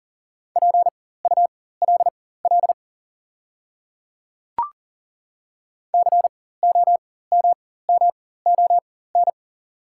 Morse code, heard as a keyed tone, about 37 wpm, spaced as for 18 wpm, 690 Hz. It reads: PULL COMMON
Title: 4x Character Spacing - Top 500 Words - 40wpm